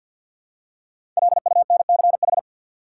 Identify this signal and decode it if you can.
Morse code, heard as a keyed tone, about 46 wpm, 700 Hz. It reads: LUNCH
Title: Five Letter Words - 50wpm